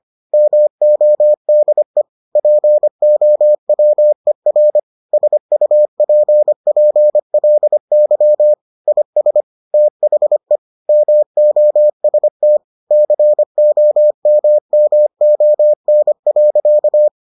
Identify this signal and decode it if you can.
Morse code, heard as a keyed tone, about 25 wpm, 600 Hz. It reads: MODE POWER SUPPLY IS THE MOST COMMON.